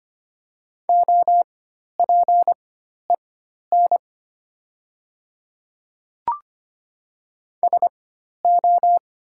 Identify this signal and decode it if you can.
Morse code, heard as a keyed tone, about 25 wpm, spaced as for 12 wpm, 700 Hz. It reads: OPEN SO
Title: 4x Character Spacing - Top 300 Words - 25wpm